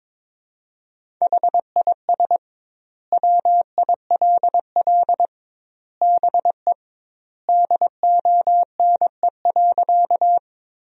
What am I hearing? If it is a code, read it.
Morse code, heard as a keyed tone, about 22 wpm, 700 Hz. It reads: HIS WILL BE DONE.
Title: 2x Word-Spacing - Sentences from Top 500 Words - 22wpm